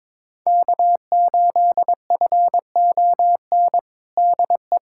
Morse code, heard as a keyed tone, about 22 wpm, 700 Hz. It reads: K8FON DE
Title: USA Callsign DE Callsign - 5 Characters - 22wpm